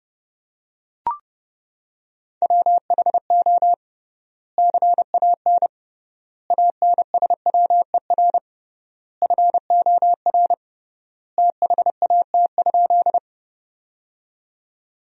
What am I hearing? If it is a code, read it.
Morse code, heard as a keyed tone, about 32 wpm, 700 Hz. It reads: WHO CAN ANSWER FOR THAT?
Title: 3x Word-Spacing - Sentences from Top 300 Words - 30wpm